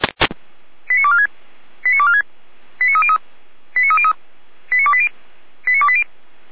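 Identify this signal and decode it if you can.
Five-tone selective call (ZVEI), heard as quick runs of five tones, tones of about 70 ms. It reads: 89246 89246 89303 89303 89280 89280
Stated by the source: ZVEI five-tone call received on SDR by feuerwehr-orlamuende fire service.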